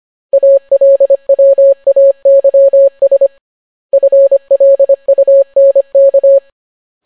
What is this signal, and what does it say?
Morse code, heard as a keyed tone, about 25 wpm, 550 Hz. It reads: ALWAYS FLUNK